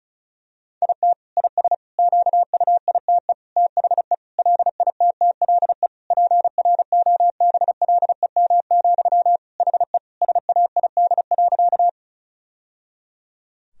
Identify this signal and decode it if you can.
Morse code, heard as a keyed tone, about 36 wpm, 700 Hz. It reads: IT IS QUITE THE LITTLE PROBLEM, HE SAID.
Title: Sentences from Top 1000 Words - Review Part 2 - 35wpm